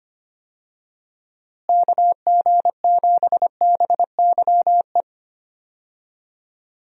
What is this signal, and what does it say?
Morse code, heard as a keyed tone, about 25 wpm, 700 Hz. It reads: KG7BYE
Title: USA Call Signs - 6 Characters - Repeated 2x - 25wpm